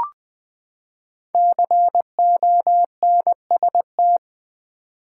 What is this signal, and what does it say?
Morse code, heard as a keyed tone, about 20 wpm, 700 Hz. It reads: CONST
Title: Build A Word - 5 Letters At A Time - 12-Letter Words - 20wpm